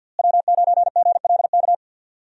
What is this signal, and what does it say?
Morse code, heard as a keyed tone, about 47 wpm, 700 Hz. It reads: W9GLK
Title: USA Call Signs - 5 Characters - Repeated 2x - 50wpm